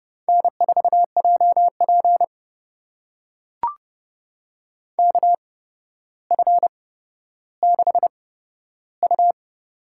Morse code, heard as a keyed tone, about 30 wpm, spaced as for 13 wpm, 700 Hz. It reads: N4JP KF6U
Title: USA Call Signs - 4 Characters - 8x Character Spacing - 30wpm